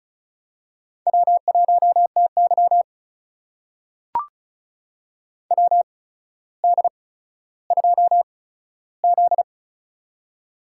Morse code, heard as a keyed tone, about 36 wpm, spaced as for 15 wpm, 700 Hz. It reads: W1TY WD2Z
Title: USA Call Signs - 4 Characters - 8x Character Spacing - 35wpm